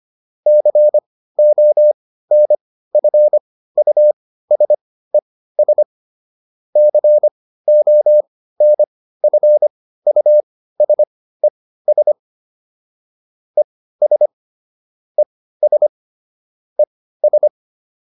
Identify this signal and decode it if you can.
Morse code, heard as a keyed tone, about 25 wpm, spaced as for 15 wpm, 600 Hz. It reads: CONFUSES CONFUSES ES ES ES